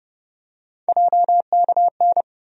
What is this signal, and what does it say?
Morse code, heard as a keyed tone, about 30 wpm, 700 Hz. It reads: JKN
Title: USA Call Sign Postfixes - 3 Letters 30wpm